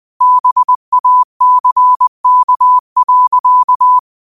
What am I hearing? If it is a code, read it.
Morse code, heard as a keyed tone, about 20 wpm, 1000 Hz. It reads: BACK.
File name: qotdfiles / 20wpm